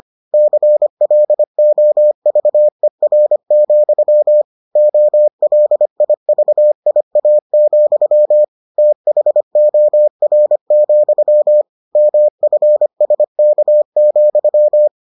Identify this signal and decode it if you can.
Morse code, heard as a keyed tone, about 25 wpm, 600 Hz. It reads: CLOVER, OLIVIA, THOR, MFSK,